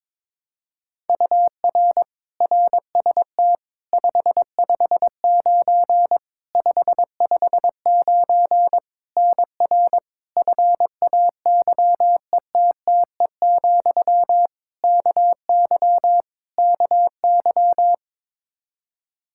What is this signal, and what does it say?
Morse code, heard as a keyed tone, about 22 wpm, 700 Hz. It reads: UR RST 559 559 NR FAYETTE, KY KY
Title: Practice QSO - Exchange 1 - Beginning - 22wpm